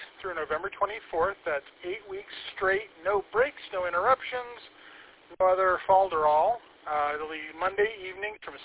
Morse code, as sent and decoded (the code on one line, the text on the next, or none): none